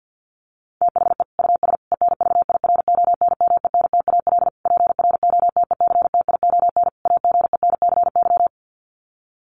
Morse code, heard as a keyed tone, about 50 wpm, 700 Hz. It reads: THE US ENVIRONMENTAL PROTECTION AGENCY
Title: Sets of 5 Words Encore2 50wpm